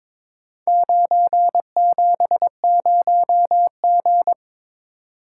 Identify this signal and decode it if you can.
Morse code, heard as a keyed tone, about 22 wpm, 700 Hz. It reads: 970G
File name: practice / 22wpm / Four Letter-Number Combinations - 22wpm